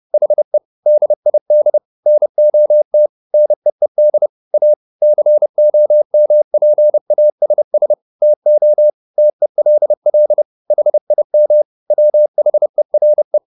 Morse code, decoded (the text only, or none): HE DID NOT NEED A COMPASS TO TELL HIM WHERE